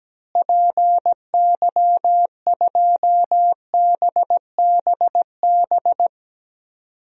Morse code, heard as a keyed tone, about 17 wpm, 700 Hz. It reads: PY2BBB
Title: DXCC-100 Call Signs - 6 Characters - Repeated 2x - 17wpm